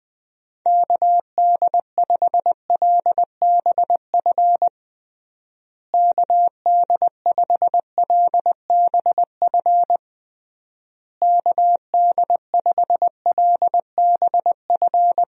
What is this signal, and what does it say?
Morse code, heard as a keyed tone, about 21 wpm, 700 Hz. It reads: KD5LBF KD5LBF KD5LBF
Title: USA Call Signs - 6 Characters - Repeated 3x - 20wpm